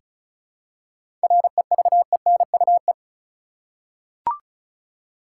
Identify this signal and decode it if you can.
Morse code, heard as a keyed tone, about 36 wpm, 700 Hz. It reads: REVENUE
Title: Seven Letter Words - 35wpm